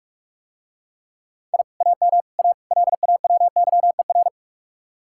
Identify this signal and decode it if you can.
Morse code, heard as a keyed tone, about 45 wpm, 700 Hz. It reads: I AM A LAWYER